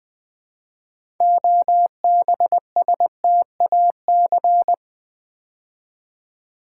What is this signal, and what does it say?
Morse code, heard as a keyed tone, about 20 wpm, 700 Hz. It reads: OBSTAC
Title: Build A Word - 2 Letters At A Time - 8-Letter Words - 20wpm